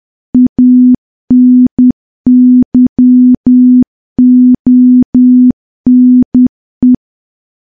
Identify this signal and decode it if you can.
Morse code, heard as a keyed tone, about 10 wpm, 250 Hz. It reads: ANYONE
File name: morse / morse-training-10WPM